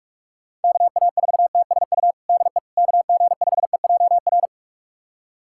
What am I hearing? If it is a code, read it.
Morse code, heard as a keyed tone, about 46 wpm, 700 Hz. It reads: KA4TSU DE KG5EJR
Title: USA Callsign DE Callsign - 6 Characters - 45wpm